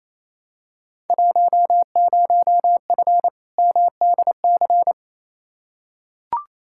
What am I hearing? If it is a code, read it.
Morse code, heard as a keyed tone, about 28 wpm, 700 Hz. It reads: 10F MDC
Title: Field Day Exchanges Encore - 28wpm